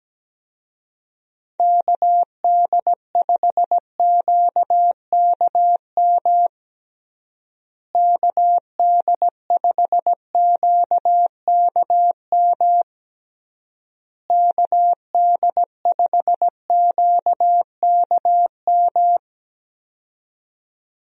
Morse code, heard as a keyed tone, about 17 wpm, 700 Hz. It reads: KD5QKM KD5QKM KD5QKM
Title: USA Call Signs - 6 Characters - Repeated 3x - 17wpm